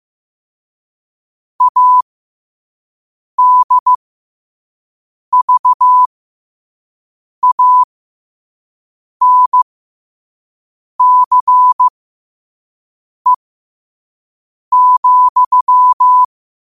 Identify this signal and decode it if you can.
Morse code, heard as a keyed tone, about 15 wpm, spaced as for 5 wpm, 1000 Hz. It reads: ADVANCE,
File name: qotdfiles / 5wpm